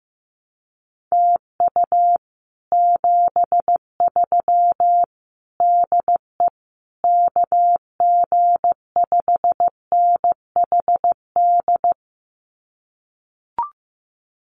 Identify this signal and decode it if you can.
Morse code, heard as a keyed tone, about 15 wpm, 700 Hz. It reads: TU 73 DE KG5NHD
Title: Common QSO Phrases Encore 15wpm